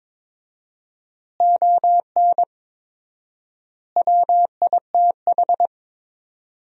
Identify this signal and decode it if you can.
Morse code, heard as a keyed tone, about 21 wpm, 700 Hz. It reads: ON WITH
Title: 4x Word-Spacing - Sentences from Top 300 Words - 22wpm